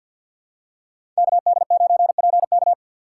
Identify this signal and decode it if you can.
Morse code, heard as a keyed tone, about 48 wpm, 700 Hz. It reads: KD9PK
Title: USA Call Signs - 5 Characters - 2x Character Spacing - 50wpm